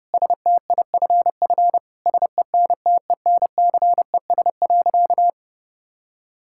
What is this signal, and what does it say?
Morse code, heard as a keyed tone, about 32 wpm, 700 Hz. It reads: STIFF SENTENCES.
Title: Pangram Challenge v2 - 30wpm